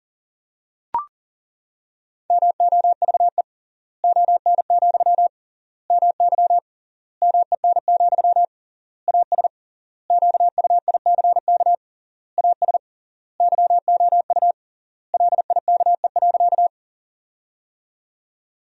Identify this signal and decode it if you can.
Morse code, heard as a keyed tone, about 40 wpm, 700 Hz. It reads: MOVE ON, MY MEN, AS QUICK AS YOU LIKE.
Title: 3x Word-Spacing - Sentences from Top 500 Words - 40wpm